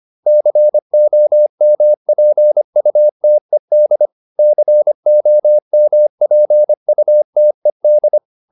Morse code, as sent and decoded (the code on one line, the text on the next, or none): -.-. --- -- .--. ..- - . -.. / -.-. --- -- .--. ..- - . -..
COMPUTED COMPUTED